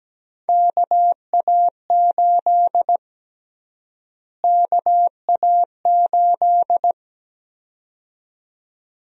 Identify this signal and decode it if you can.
Morse code, heard as a keyed tone, about 17 wpm, 700 Hz. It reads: KA8 KA8
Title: USA Call Sign Prefixes - Repeated 2x - 17wpm